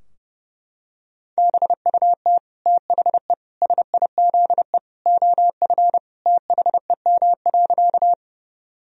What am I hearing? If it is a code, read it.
Morse code, heard as a keyed tone, about 30 wpm, 700 Hz. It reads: BUT THE SIZE OF THEM.
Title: Sentences from Top 1000 Words - Review Part 2 - 30wpm